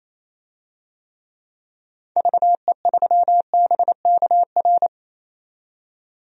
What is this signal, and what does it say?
Morse code, heard as a keyed tone, about 28 wpm, 700 Hz. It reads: VE3BKR
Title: DXCC-100 Call Signs - 6 Characters - 28wpm